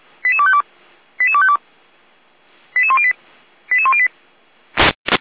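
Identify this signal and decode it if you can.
Five-tone selective call (ZVEI), heard as quick runs of five tones, tones of about 80 ms. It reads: 88252 88252 88108 88108
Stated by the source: ZVEI five-tone call received with an SDR.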